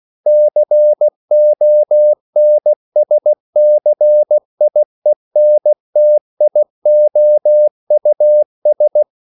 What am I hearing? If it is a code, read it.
Morse code, heard as a keyed tone, about 16 wpm, 600 Hz. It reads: CONSCIENTIOUS
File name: CW / Practice / EnglishQuotations0116